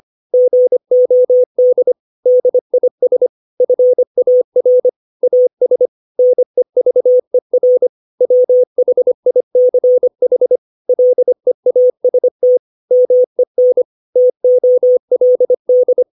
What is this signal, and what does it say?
Morse code, heard as a keyed tone, about 25 wpm, 500 Hz. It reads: GOD DIS FAR AS NEVER WHICH LEAST MEN TOLD
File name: 632 words / 632 Words 25WPM